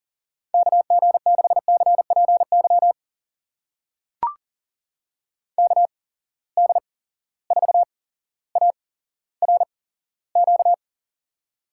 Morse code, heard as a keyed tone, about 40 wpm, spaced as for 17 wpm, 700 Hz. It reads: KG6CPY KD4ARQ
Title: USA Call Signs - 6 Characters - 8x Character Spacing - 40wpm